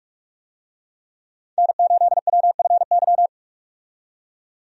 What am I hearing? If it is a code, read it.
Morse code, heard as a keyed tone, about 43 wpm, 700 Hz. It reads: N8WFY
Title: USA Call Signs - 5 Characters - 45wpm